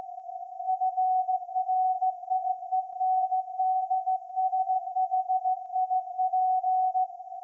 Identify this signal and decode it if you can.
Morse code, heard as a keyed tone, about 16 wpm, 750 Hz. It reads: FRIENDSHIP
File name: iz7slz / VLF / SAQ 30-06-2024 0900